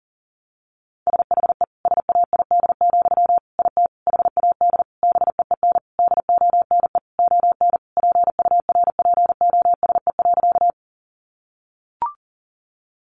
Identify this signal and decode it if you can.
Morse code, heard as a keyed tone, about 40 wpm, 700 Hz. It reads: SHE SAID, IT HAD BEEN DONE ON PURPOSE.
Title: Sentences from Top 800 Words 40wpm